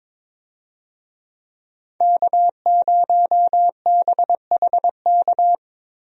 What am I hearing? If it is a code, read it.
Morse code, heard as a keyed tone, about 22 wpm, 700 Hz. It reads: K0BHK
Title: USA Call Signs - 5 Characters - Repeated 3x - 22wpm